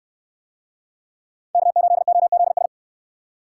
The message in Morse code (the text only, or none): K8OZS